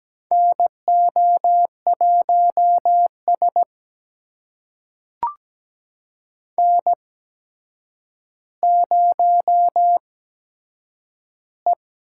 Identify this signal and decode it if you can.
Morse code, heard as a keyed tone, about 17 wpm, spaced as for 8 wpm, 700 Hz. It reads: NO1S N0E
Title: USA Call Signs - 4 Characters - 8x Character Spacing - 17wpm